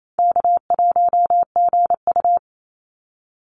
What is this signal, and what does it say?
Morse code, heard as a keyed tone, about 27 wpm, 700 Hz. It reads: K1GU
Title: Call Signs - 28wpm